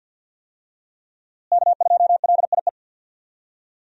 Morse code, heard as a keyed tone, about 50 wpm, 700 Hz. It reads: K2LIE
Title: USA Call Signs - 5 Characters - Repeated 2x - 50wpm